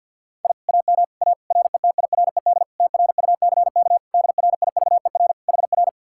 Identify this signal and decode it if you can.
Morse code, heard as a keyed tone, about 60 wpm, 700 Hz. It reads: I AM A RETIRED TRUCK DRIVER HR
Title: Speed-Racing Common QSO Phrases - 50wpm